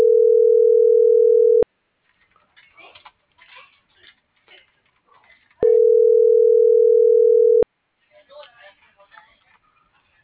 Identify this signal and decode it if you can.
Ringback tone (North America), 440 + 480 Hz.
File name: CENTRAL SOTOVA / out alejandro.rodriguez - .57823